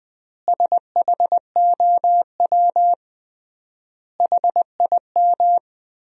Morse code, heard as a keyed tone, about 20 wpm, 700 Hz. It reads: SHOW HIM
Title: 3x Word-Spacing - Sentences from Top 200 Words - 20wpm